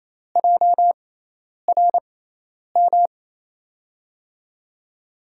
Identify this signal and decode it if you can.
Morse code, heard as a keyed tone, about 28 wpm, spaced as for 10 wpm, 700 Hz. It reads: JRM
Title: USA Call Signs - 5 Characters - 6x Character Spacing - 28wpm